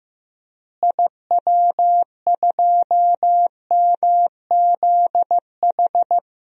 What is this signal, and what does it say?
Morse code, heard as a keyed tone, about 15 wpm, 700 Hz. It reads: IW2MZH
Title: DXCC-100 Call Signs - 6 Characters - Repeated 3x - 15wpm